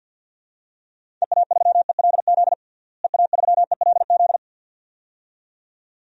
Morse code, heard as a keyed tone, about 46 wpm, 700 Hz. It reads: EA3ELZ EA3ELZ
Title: DXCC-100 Call Signs - 6 Characters - Repeated 2x - 50wpm